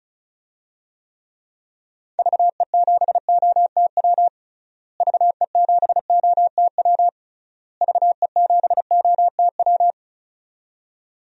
Morse code, heard as a keyed tone, about 36 wpm, 700 Hz. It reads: VE7OTW VE7OTW VE7OTW
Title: DXCC-100 Call Signs - 6 Characters - Repeated 3x - 35wpm